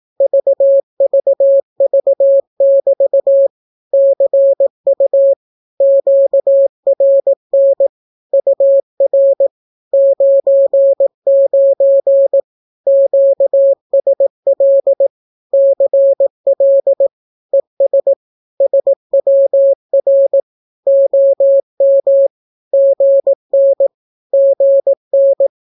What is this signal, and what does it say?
Morse code, heard as a keyed tone, about 18 wpm, 550 Hz. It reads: VVV<BT> CU QRN UR 99 QSL CL ES SWR OM GN GN
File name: naukapodstaw / skroty / skroty